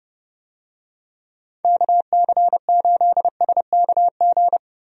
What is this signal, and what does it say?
Morse code, heard as a keyed tone, about 30 wpm, 700 Hz. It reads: KC8SKG
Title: USA Call Signs - 6 Characters - Repeated 2x - 30wpm